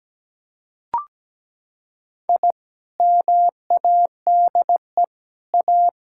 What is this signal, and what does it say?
Morse code, heard as a keyed tone, about 17 wpm, 700 Hz. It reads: I MADE A
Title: Sentences from Top 1000 Words - Review Part 6 - 17wpm